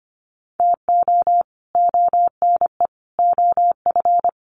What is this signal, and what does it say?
Morse code, heard as a keyed tone, about 25 wpm, 700 Hz. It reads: TO ONE OF
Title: Sets of 4 Words 25wpm